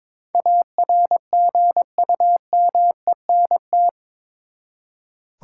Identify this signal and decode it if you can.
Morse code, heard as a keyed tone, about 22 wpm, 700 Hz. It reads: ARGUMENT
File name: practice / 22wpm / Eight Letter Words - 22wpm